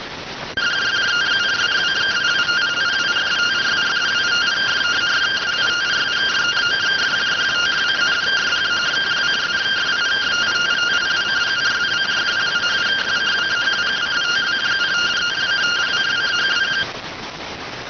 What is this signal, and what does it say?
Two-tone teleprinter signal, about 170 Hz shift.